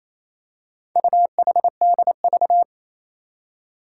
Morse code, heard as a keyed tone, about 27 wpm, 700 Hz. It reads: UHDV
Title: Four Challenging Letter-Number Combinations 28wpm